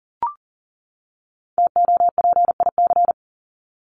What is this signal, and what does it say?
Morse code, heard as a keyed tone, about 37 wpm, 700 Hz. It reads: TOPIC
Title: Top 1001-1250 Words 40wpm